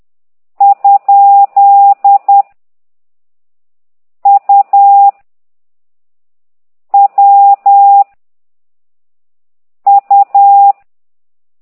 Morse code, heard as a keyed tone, about 10 wpm, 800 Hz. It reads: ? U W U
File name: ct2idw / exerc8